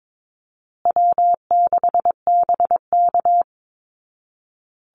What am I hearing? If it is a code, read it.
Morse code, heard as a keyed tone, about 22 wpm, 700 Hz. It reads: W6BK
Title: Easy Call Signs Repeated 3x - 22wpm